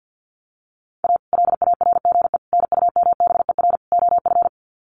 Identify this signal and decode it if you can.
Morse code, heard as a keyed tone, about 50 wpm, 700 Hz. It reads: A LARGE NUMBER OF
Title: Sets of 4 Words 50wpm